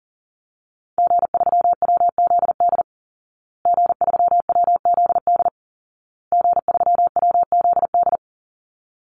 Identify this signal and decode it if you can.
Morse code, heard as a keyed tone, about 40 wpm, 700 Hz. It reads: G3WZD G3WZD G3WZD
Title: Call Signs Repeated 2x - 40wpm